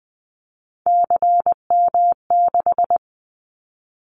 Morse code, heard as a keyed tone, about 20 wpm, 700 Hz. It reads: CM6
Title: Callsign Prefixes for Top 100 DXCC - 20wpm